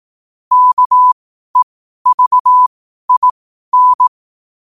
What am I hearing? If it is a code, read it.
Morse code, heard as a keyed tone, about 18 wpm, spaced as for 12 wpm, 1000 Hz. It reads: KEVIN
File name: 12wpm